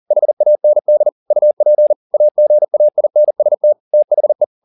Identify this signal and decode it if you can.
Morse code, heard as a keyed tone, about 40 wpm, 600 Hz. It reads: HAND UP AGAINST THE